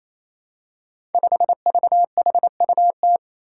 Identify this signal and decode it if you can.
Morse code, heard as a keyed tone, about 27 wpm, 700 Hz. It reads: 5VHUT
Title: Five Challenging Letter-Number Combinations 28wpm